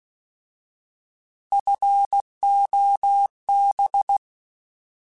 Morse code, heard as a keyed tone, about 16 wpm, 800 Hz. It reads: FOB
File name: Alphabetical / Alphabetical mp3s / Fs @15